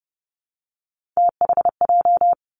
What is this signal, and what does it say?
Morse code, heard as a keyed tone, about 31 wpm, 700 Hz. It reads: THJ
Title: Three Challenging Letter-Number Combinations 30wpm